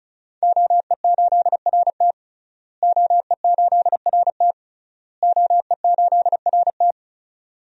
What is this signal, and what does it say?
Morse code, heard as a keyed tone, about 36 wpm, 700 Hz. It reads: OE8RT OE8RT OE8RT
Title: DXCC-100 Call Signs - 5 Characters - Repeated 3x - 35wpm